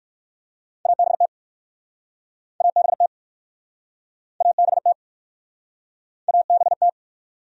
Morse code, heard as a keyed tone, about 55 wpm, 700 Hz. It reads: ABT ABT ABT ABT
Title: Speed-Racing Abbreviations - 45wpm